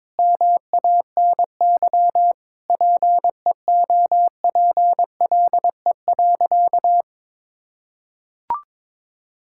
Morse code, heard as a keyed tone, about 23 wpm, 700 Hz. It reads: MANY PEOPLE.
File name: Difficult Sentences with Hints - 22wpm